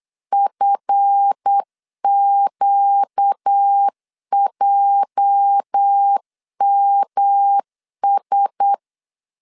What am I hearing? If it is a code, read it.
Morse code, heard as a keyed tone, about 8 wpm, 800 Hz. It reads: FQJMS
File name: LEZIONI / 10 SIGNS / 10A SLASH 8wpm 3452